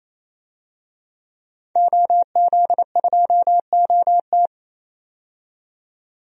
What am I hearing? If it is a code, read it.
Morse code, heard as a keyed tone, about 27 wpm, 700 Hz. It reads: OZ2OT
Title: DXCC-100 Call Signs - 5 Characters - 28wpm